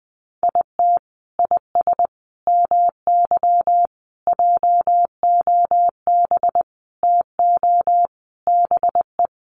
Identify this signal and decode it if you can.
Morse code, heard as a keyed tone, about 20 wpm, 700 Hz. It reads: IT IS MY JOB TO BE
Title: Sentences from Top 600 Words 20wpm